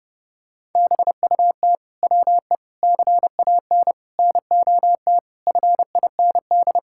Morse code, heard as a keyed tone, about 32 wpm, 700 Hz. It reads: BUT WE CAN NOT FIND
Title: Sentences from Top 1000 Words - Review Part 5 - 30wpm